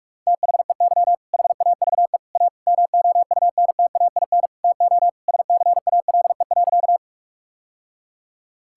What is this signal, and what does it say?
Morse code, heard as a keyed tone, about 41 wpm, 700 Hz. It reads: THEY HAVE A MOUNTAIN TO SCALE.
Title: Sentences from Top 1000 Words - Review Part 5 - 45wpm